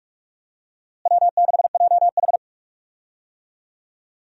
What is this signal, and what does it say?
Morse code, heard as a keyed tone, about 42 wpm, 700 Hz. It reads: W6JH
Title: USA Call Signs - 4 Characters - Repeated 3x - 45wpm